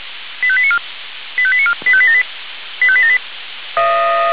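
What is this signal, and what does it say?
Five-tone selective call (ZVEI), heard as quick runs of five tones, tones of about 70 ms. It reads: 85894 85894 85878 85878
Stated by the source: Received via SDR as a ZVEI five-tone call, address 85201.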